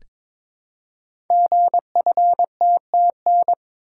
Morse code, heard as a keyed tone, about 23 wpm, 700 Hz. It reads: GFTTN
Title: Code Groups - 5 Letters - 22wpm